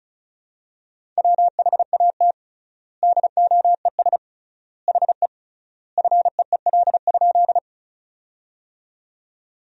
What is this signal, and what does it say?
Morse code, heard as a keyed tone, about 36 wpm, 700 Hz. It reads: WHAT DOES HE FEEL?
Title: 3x Word-Spacing - Sentences from Top 400 Words - 35wpm